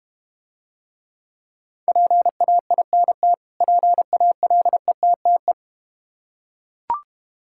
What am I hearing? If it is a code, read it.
Morse code, heard as a keyed tone, about 31 wpm, 700 Hz. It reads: PAINT PALETTE